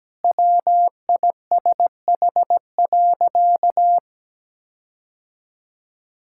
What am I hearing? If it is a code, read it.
Morse code, heard as a keyed tone, about 17 wpm, 700 Hz. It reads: WISH.
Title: Sentences from Top 1600 Words 17wpm